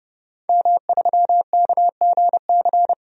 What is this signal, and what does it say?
Morse code, heard as a keyed tone, about 32 wpm, 700 Hz. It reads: M3KGC
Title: DXCC-100 Call Signs - 5 Characters - 30wpm